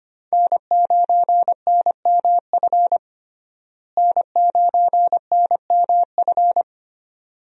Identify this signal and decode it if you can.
Morse code, heard as a keyed tone, about 25 wpm, 700 Hz. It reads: N9NMF N9NMF
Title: USA Call Signs - 5 Characters - Repeated 2x - 25wpm